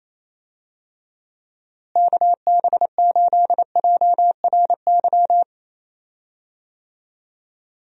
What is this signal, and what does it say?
Morse code, heard as a keyed tone, about 28 wpm, 700 Hz. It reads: KB8JRY
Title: USA Call Signs - 6 Characters - Repeated 2x - 28wpm